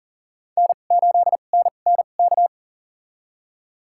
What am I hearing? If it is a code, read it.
Morse code, heard as a keyed tone, about 40 wpm, spaced as for 29 wpm, 700 Hz. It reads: N8NNK
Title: USA Call Signs - 2x Character Spacing - 40wpm